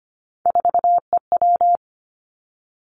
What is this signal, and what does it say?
Morse code, heard as a keyed tone, about 25 wpm, 700 Hz. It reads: 4EW
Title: Three Challenging Letter-Number Combinations 25wpm